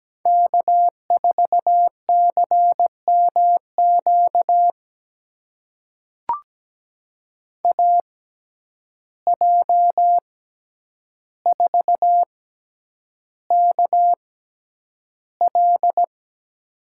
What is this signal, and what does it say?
Morse code, heard as a keyed tone, about 17 wpm, spaced as for 9 wpm, 700 Hz. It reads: K4CMQ AJ4KL
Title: USA Call Signs - 5 Characters - 6x Character Spacing - 17wpm